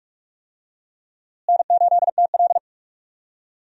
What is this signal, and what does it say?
Morse code, heard as a keyed tone, about 41 wpm, 700 Hz. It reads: N8TL
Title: USA Call Signs - 4 Characters - Repeated 3x - 45wpm